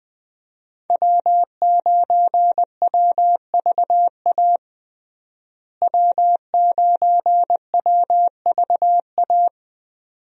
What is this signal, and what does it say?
Morse code, heard as a keyed tone, about 21 wpm, 700 Hz. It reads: W9WVA W9WVA
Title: USA Call Signs - 5 Characters - Repeated 2x - 20wpm